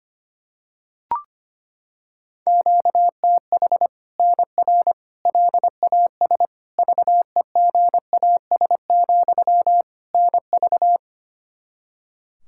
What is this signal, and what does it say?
Morse code, heard as a keyed tone, about 25 wpm, 700 Hz. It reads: QTH NR LAS VEGAS, NV
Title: Practice QSO - Exchange 1 - Beginning - 25wpm